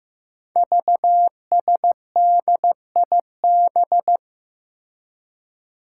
Morse code, heard as a keyed tone, about 15 wpm, 700 Hz. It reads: VSDIB
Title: Five Challenging Letter-Number Combinations 15wpm